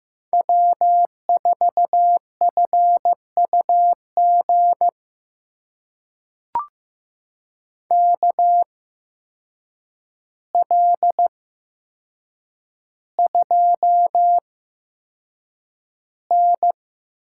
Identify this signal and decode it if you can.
Morse code, heard as a keyed tone, about 15 wpm, spaced as for 7 wpm, 700 Hz. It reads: W4FUG KL2N
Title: USA Call Signs - 8x Character Spacing - 15wpm